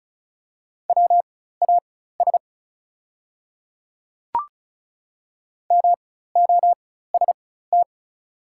Morse code, heard as a keyed tone, about 34 wpm, spaced as for 16 wpm, 700 Hz. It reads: WAS MOST
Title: 4x Character Spacing - Top 100 Words - 35wpm